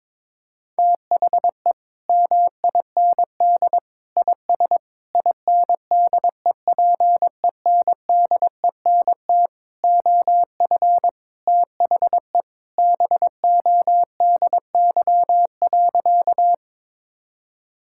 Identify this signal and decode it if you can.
Morse code, heard as a keyed tone, about 22 wpm, 700 Hz. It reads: THE MIND IS INDEPENDENT OF THE BODY.
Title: Sentences from Top 1000 Words - Review Part 6 - 22wpm